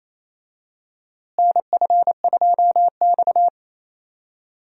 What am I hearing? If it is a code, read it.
Morse code, heard as a keyed tone, about 28 wpm, 700 Hz. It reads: NF2X
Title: USA Call Signs - 4 Characters - Repeated 2x - 28wpm